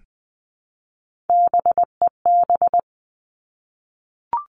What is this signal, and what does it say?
Morse code, heard as a keyed tone, about 20 wpm, 700 Hz. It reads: BEB
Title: All Cut Numbers - 3-Digit 20wpm